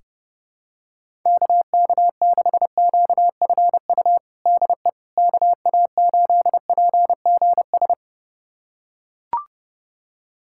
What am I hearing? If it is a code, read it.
Morse code, heard as a keyed tone, about 30 wpm, 700 Hz. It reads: KK6QFU DE KA8PGS
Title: USA Callsign DE Callsign - 6 Characters - 30wpm